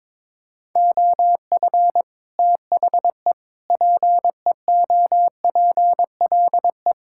Morse code, heard as a keyed tone, about 22 wpm, 700 Hz. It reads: OF THE PEOPLE
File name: practice / 22wpm / Sets of 8 Words - 22wpm